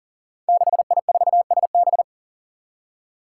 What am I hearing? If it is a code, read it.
Morse code, heard as a keyed tone, about 40 wpm, 700 Hz. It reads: 6I4SB
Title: Five Challenging Letter-Number Combinations 40wpm